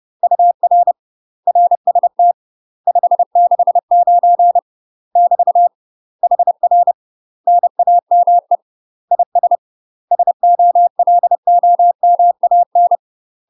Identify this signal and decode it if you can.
Morse code, heard as a keyed tone, about 30 wpm, 700 Hz. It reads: UR RST 569 <BT> HR NAME IS SOLOMAN
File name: UK MASTER QSOs TRAINING/30WPM